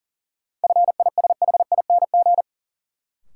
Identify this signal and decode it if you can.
Morse code, heard as a keyed tone, about 40 wpm, 700 Hz. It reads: FISHING
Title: Sentences from Top 1000 Words - Review Part 5 - New Word Drill - 40wpm